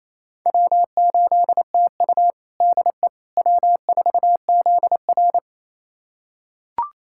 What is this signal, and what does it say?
Morse code, heard as a keyed tone, about 27 wpm, 700 Hz. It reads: W8TU DE W4ZR
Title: USA Callsign DE Callsign - 4 Characters - 28wpm